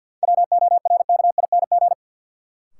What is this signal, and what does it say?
Morse code, heard as a keyed tone, about 53 wpm, 700 Hz. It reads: WORKING
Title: Sentences from Top 700 Words - New Word Drill - 50wpm